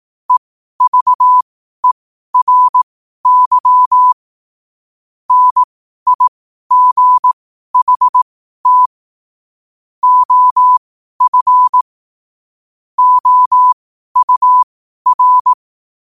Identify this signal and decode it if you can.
Morse code, heard as a keyed tone, about 18 wpm, spaced as for 12 wpm, 1000 Hz. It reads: EVERY NIGHT OF OUR
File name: qotdfiles / 12wpm